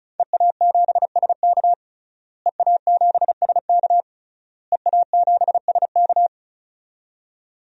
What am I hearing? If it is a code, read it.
Morse code, heard as a keyed tone, about 35 wpm, 700 Hz. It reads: EA7SK EA7SK EA7SK
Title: DXCC-100 Call Signs - 5 Characters - Repeated 3x - 35wpm